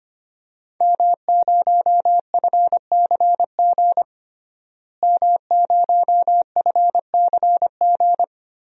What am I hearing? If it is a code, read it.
Morse code, heard as a keyed tone, about 25 wpm, 700 Hz. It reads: M0FCG M0FCG
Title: DXCC-100 Call Signs - 5 Characters - Repeated 2x - 25wpm